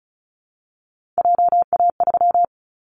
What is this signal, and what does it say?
Morse code, heard as a keyed tone, about 34 wpm, 700 Hz. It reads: JA3
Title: Callsign Prefixes for Top 25 DXCC - Repeated 3x - 35wpm